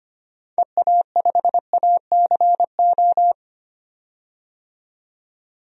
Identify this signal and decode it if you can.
Morse code, heard as a keyed tone, about 25 wpm, 700 Hz. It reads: EA5ACO
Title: DXCC-100 Call Signs - 6 Characters - 25wpm